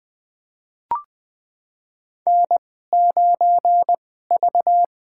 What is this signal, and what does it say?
Morse code, heard as a keyed tone, about 20 wpm, spaced as for 14 wpm, 700 Hz. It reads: N9V
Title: USA Call Signs - 2x Character Spacing - 20wpm